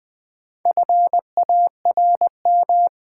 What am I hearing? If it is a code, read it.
Morse code, heard as a keyed tone, about 20 wpm, 700 Hz. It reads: FARM
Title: Sentences from Top 1000 Words - Review Part 2 - New Word Drill - 20wpm